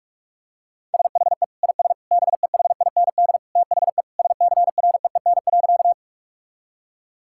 Morse code, heard as a keyed tone, about 41 wpm, 700 Hz. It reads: SHE IS BEHIND THE SCREEN.
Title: Sentences from Top 1000 Words - Review Part 6 - 45wpm